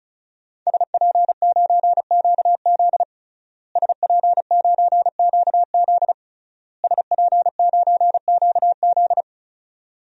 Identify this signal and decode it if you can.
Morse code, heard as a keyed tone, about 36 wpm, 700 Hz. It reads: SP9QZ SP9QZ SP9QZ
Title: DXCC-100 Call Signs - 5 Characters - Repeated 3x - 35wpm